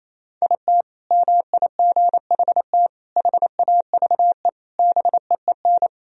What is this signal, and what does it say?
Morse code, heard as a keyed tone, about 28 wpm, 700 Hz. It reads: IT MIGHT HAVE BEEN
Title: Sentences from Top 1000 Words - Review Part 3 - 28wpm